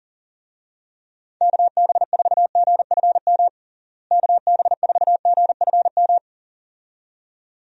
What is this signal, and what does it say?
Morse code, heard as a keyed tone, about 40 wpm, 700 Hz. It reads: KB4GFM KB4GFM
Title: USA Call Signs - 6 Characters - Repeated 2x - 40wpm